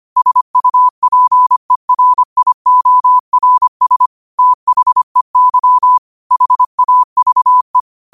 Morse code, heard as a keyed tone, about 25 wpm, 1000 Hz. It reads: SUPERIORS THEY HAVE
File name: qotdfiles / 25wpm